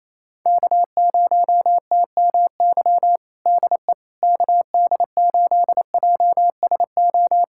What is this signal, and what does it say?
Morse code, heard as a keyed tone, about 28 wpm, 700 Hz. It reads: K0TMY DE KD8JSO
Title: USA Callsign DE Callsign 28wpm